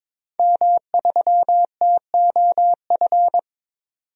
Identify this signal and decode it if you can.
Morse code, heard as a keyed tone, about 22 wpm, 700 Hz. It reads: M3TOF
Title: DXCC-100 Call Signs - 5 Characters - 22wpm